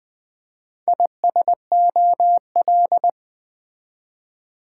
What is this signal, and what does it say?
Morse code, heard as a keyed tone, about 21 wpm, 700 Hz. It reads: ISOL
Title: Build A Word - 2 Letters At A Time - 8-Letter Words - 20wpm